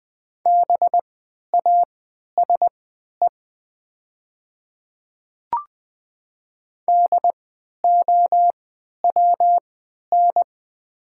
Morse code, heard as a keyed tone, about 20 wpm, spaced as for 11 wpm, 700 Hz. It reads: BASE DOWN
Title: 3x Character Spacing - Top 300 Words - 20wpm